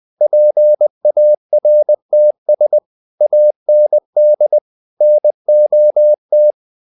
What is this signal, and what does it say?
Morse code, heard as a keyed tone, about 20 wpm, 600 Hz. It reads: PARTS AND NOT
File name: the old man and the sea part 2 20wpm